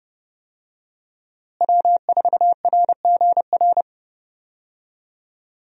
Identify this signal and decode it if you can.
Morse code, heard as a keyed tone, about 30 wpm, 700 Hz. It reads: W4RGR
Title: USA Call Signs - 5 Characters - Repeated 3x - 30wpm